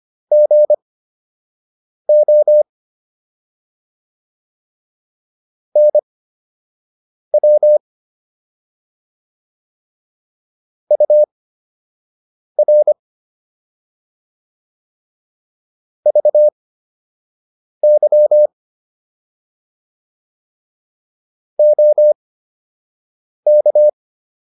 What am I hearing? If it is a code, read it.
Morse code, heard as a keyed tone, about 25 wpm, spaced as for 6 wpm, 600 Hz. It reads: GO NW UR VY OK